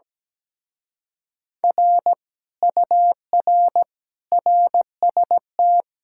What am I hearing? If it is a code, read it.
Morse code, heard as a keyed tone, about 17 wpm, 700 Hz. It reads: R UR RST
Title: Speed-Racing Common QSO Phrases - 17wpm